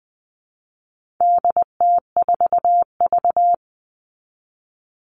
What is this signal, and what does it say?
Morse code, heard as a keyed tone, about 20 wpm, 700 Hz. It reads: DT4V
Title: All Cut Numbers - 4-Digit 20wpm